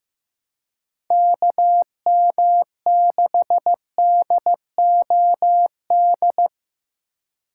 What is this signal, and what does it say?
Morse code, heard as a keyed tone, about 15 wpm, 700 Hz. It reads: KM6DOD
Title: USA Call Signs - 6 Characters - 4x Character Spacing - 15wpm